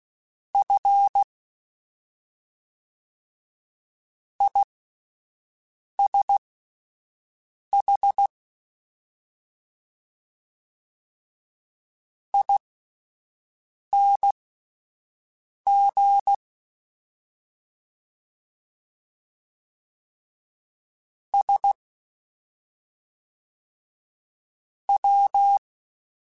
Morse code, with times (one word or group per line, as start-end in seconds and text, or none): 0.5-1.2 F
4.4-8.3 ISH
12.3-16.3 ING
21.3-21.7 S
24.9-25.6 W